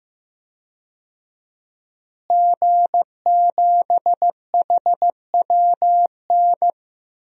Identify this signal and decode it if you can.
Morse code, heard as a keyed tone, about 15 wpm, 700 Hz. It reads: G7HWN
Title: DXCC-100 Call Signs - 5 Characters - Repeated 2x - 15wpm